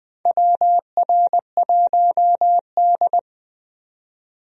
Morse code, heard as a keyed tone, about 20 wpm, 700 Hz. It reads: WR1D
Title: USA Call Signs - 4 Characters - 20wpm